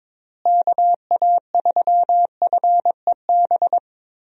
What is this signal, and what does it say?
Morse code, heard as a keyed tone, about 22 wpm, 700 Hz. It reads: KA3FEB
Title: USA Call Signs - 6 Characters - Repeated 3x - 22wpm